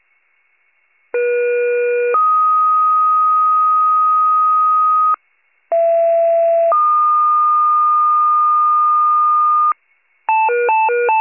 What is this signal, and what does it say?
Two-tone page, heard as a one-second tone, then a three-second tone, about 480 then 1250 Hz, about 670 then 1220 Hz. It is heard from 1.1 to 9.7 s.